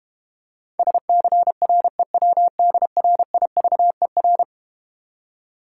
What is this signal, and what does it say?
Morse code, heard as a keyed tone, about 32 wpm, 700 Hz. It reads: SCREWDRIVER